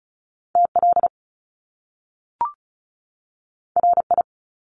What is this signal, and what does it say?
Morse code, heard as a keyed tone, about 36 wpm, 700 Hz. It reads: TL RI
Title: Top Two Letter Combinations in Words 35wpm